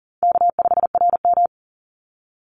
Morse code, heard as a keyed tone, about 40 wpm, 700 Hz. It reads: K5RM
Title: Easy Call Signs - 40wpm